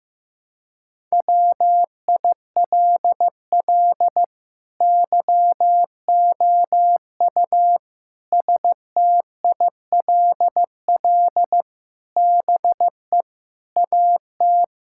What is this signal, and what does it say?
Morse code, heard as a keyed tone, about 15 wpm, 700 Hz. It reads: WILL YOU STILL BE AT
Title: Sentences from Top 1000 Words - Review Part 5 - 15wpm